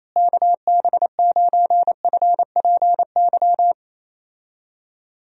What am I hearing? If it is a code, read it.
Morse code, heard as a keyed tone, about 28 wpm, 700 Hz. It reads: KB9FPY
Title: USA Call Signs - 6 Characters - Repeated 3x - 28wpm